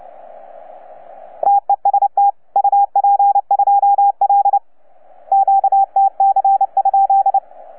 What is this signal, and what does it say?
Morse code, heard as a keyed tone, about 29 wpm, 810 Hz, with radio noise. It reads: TEST UP2L QTC?
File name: UP2L-20-CW